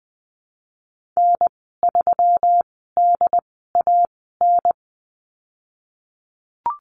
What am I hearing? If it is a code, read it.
Morse code, heard as a keyed tone, about 20 wpm, spaced as for 15 wpm, 700 Hz. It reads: N3DAN
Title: 122 Lesson 20g - Number 3 - Call Signs 20wpm